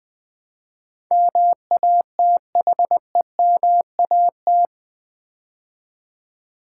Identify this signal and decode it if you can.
Morse code, heard as a keyed tone, about 21 wpm, 700 Hz. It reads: MATHEMAT
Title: Build A Word - 4 Letters At A Time - 12-Letter Words - 20wpm